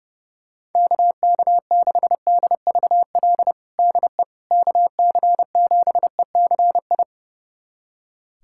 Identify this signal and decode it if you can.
Morse code, heard as a keyed tone, about 30 wpm, 700 Hz. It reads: KK6DVL DE KC7ECI
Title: USA Callsign DE Callsign - 6 Characters - 30wpm